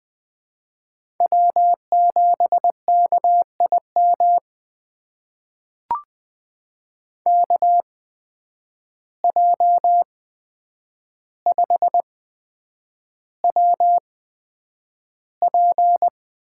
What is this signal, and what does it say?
Morse code, heard as a keyed tone, about 20 wpm, spaced as for 9 wpm, 700 Hz. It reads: W7KIM KJ5WP
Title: USA Call Signs - 5 Characters - 8x Character Spacing - 20wpm